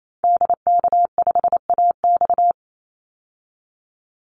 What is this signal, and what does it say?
Morse code, heard as a keyed tone, about 28 wpm, 700 Hz. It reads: DK5AX
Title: Call Signs Repeated 2x - 28wpm